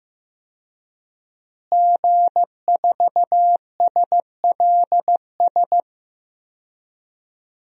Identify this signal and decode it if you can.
Morse code, heard as a keyed tone, about 15 wpm, 700 Hz. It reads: G4SLS
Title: DXCC-100 Call Signs - 5 Characters - Repeated 3x - 15wpm